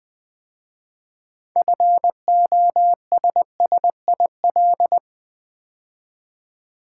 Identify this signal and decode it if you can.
Morse code, heard as a keyed tone, about 20 wpm, 700 Hz. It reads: FOSSIL
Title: 6 Letter Words - First 3 Letters Given - 20wpm